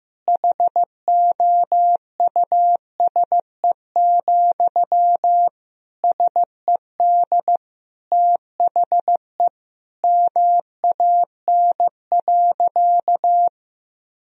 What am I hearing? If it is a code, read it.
Morse code, heard as a keyed tone, about 15 wpm, 700 Hz. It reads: HOUSE, SED THE MAN.
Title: Sentences from Top 400 Words and Abbreviations 15wpm